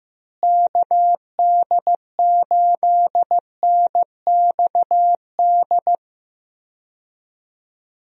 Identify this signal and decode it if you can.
Morse code, heard as a keyed tone, about 15 wpm, 700 Hz. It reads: KD8NXD
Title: USA Call Signs - 6 Characters - Repeated 3x - 15wpm